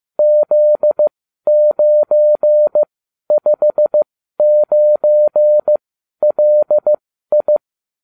Morse code, heard as a keyed tone, about 15 wpm, spaced as for 12 wpm, 600 Hz. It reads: Z959LI